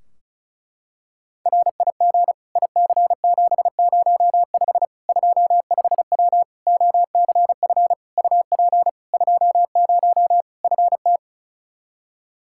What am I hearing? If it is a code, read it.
Morse code, heard as a keyed tone, about 34 wpm, 700 Hz. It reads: RIG IC705 25W OCF UP 20 FT
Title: K1USN SST QSO Practice 35wpm